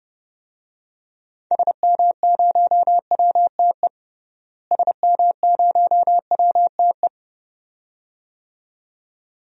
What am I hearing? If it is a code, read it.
Morse code, heard as a keyed tone, about 30 wpm, 700 Hz. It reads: SM0WTE SM0WTE
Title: DXCC-100 Call Signs - 6 Characters - Repeated 2x - 30wpm